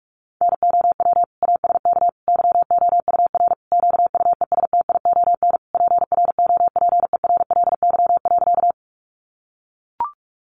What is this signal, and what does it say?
Morse code, heard as a keyed tone, about 44 wpm, 700 Hz. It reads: NOW ASK YOUR QUESTION PROPERLY.
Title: Sentences from Top 1100 Words 45wpm